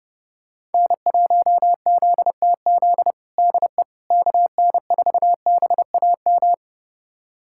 Morse code, heard as a keyed tone, about 30 wpm, 700 Hz. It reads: N1ZTZ DE KN4BAM
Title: USA Callsign DE Callsign 30wpm